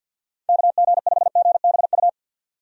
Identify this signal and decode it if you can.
Morse code, heard as a keyed tone, about 51 wpm, 700 Hz. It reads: KG5GBU
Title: USA Call Signs - 6 Characters - 50wpm